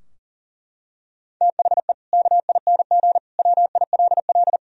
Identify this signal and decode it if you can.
Morse code, heard as a keyed tone, about 40 wpm, 700 Hz. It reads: THE KING WILL
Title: Sentences from Top 1000 Words - Review Part 3 - 40wpm